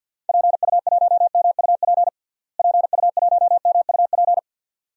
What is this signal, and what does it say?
Morse code, heard as a keyed tone, about 49 wpm, 700 Hz. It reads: PU1MUP PU1MUP
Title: DXCC-100 Call Signs - 6 Characters - Repeated 2x - 50wpm